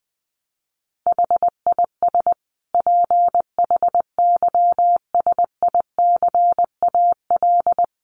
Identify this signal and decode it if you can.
Morse code, heard as a keyed tone, about 20 wpm, 700 Hz. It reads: HIS PHYSICAL
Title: Sentences from Top 1400 Words 20wpm